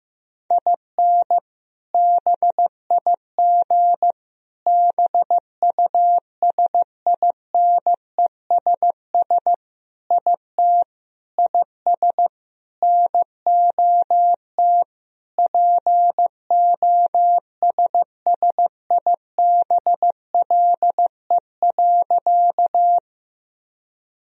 Morse code, heard as a keyed tone, about 15 wpm, 700 Hz. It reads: IN BIG BUSINESS IT IS NOT POSSIBLE.
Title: Sentences from Top 1000 Words - Review Part 1 - 15wpm